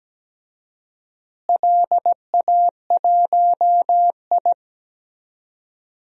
Morse code, heard as a keyed tone, about 17 wpm, 700 Hz. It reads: LA1I
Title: DXCC-100 Call Signs - 4 Characters - Repeated 3x - 17wpm